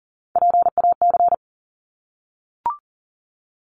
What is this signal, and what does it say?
Morse code, heard as a keyed tone, about 40 wpm, 700 Hz. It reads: PAC
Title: ARRL Sections 40wpm